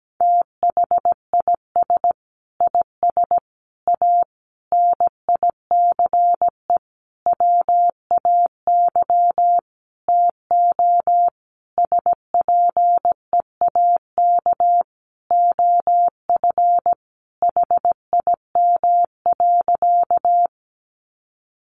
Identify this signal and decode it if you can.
Morse code, heard as a keyed tone, about 17 wpm, 700 Hz. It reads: THIS IS A NICE WAY TO SPEAK OF HIM.
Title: Sentences from Top 1100 Words 17wpm